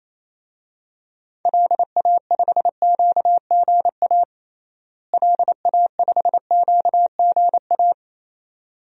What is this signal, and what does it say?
Morse code, heard as a keyed tone, about 28 wpm, 700 Hz. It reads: LA5QGA LA5QGA
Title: DXCC-100 Call Signs - 6 Characters - Repeated 2x - 28wpm